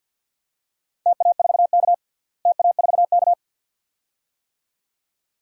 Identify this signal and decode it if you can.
Morse code, heard as a keyed tone, about 52 wpm, 700 Hz. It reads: TA4K TA4K
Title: DXCC-100 Call Signs - 4 Characters - Repeated 2x - 50wpm